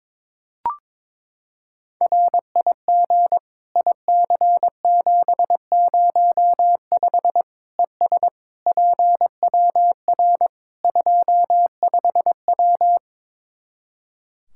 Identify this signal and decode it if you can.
Morse code, heard as a keyed tone, about 22 wpm, 700 Hz. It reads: RIG IC705 ES PWR 25W
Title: Practice QSO - Exchange 2 - Info - 22wpm